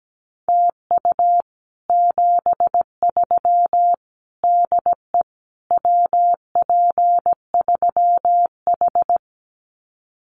Morse code, heard as a keyed tone, about 17 wpm, 700 Hz. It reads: TU 73 DE WP3H
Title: Common QSO Phrases Encore 17wpm